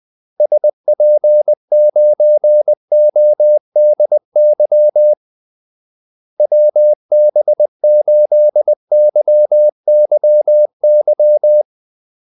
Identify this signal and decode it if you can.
Morse code, heard as a keyed tone, about 20 wpm, 600 Hz. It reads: SP9ODY WB8YYY
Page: Kaldesignaler 20 wpm